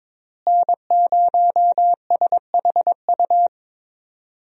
Morse code, heard as a keyed tone, about 22 wpm, 700 Hz. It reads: N0SHU
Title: USA Call Signs - 5 Characters - Repeated 2x - 22wpm